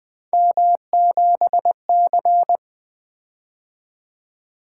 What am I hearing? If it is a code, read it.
Morse code, heard as a keyed tone, about 20 wpm, 700 Hz. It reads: M7C
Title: Three Letter-Number Combinations - 20wpm